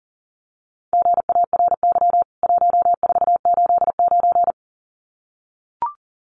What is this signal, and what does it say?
Morse code, heard as a keyed tone, about 40 wpm, 700 Hz. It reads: GARY 1489
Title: CWT Exchange 40wpm